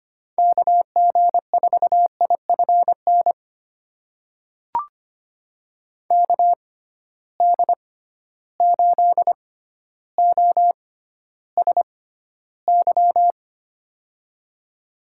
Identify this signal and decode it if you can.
Morse code, heard as a keyed tone, about 25 wpm, spaced as for 13 wpm, 700 Hz. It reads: KG4IFN KD8OSY
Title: USA Call Signs - 6 Characters - 6x Character Spacing - 25wpm